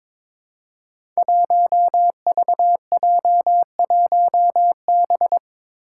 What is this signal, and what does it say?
Morse code, heard as a keyed tone, about 22 wpm, 700 Hz. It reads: 1VJ1B
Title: Five Challenging Letter-Number Combinations 22wpm